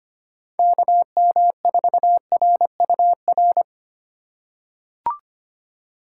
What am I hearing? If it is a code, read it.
Morse code, heard as a keyed tone, about 25 wpm, 700 Hz. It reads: KM4RUR
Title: USA Call Signs - 6 Characters - 3x Character Spacing - 25wpm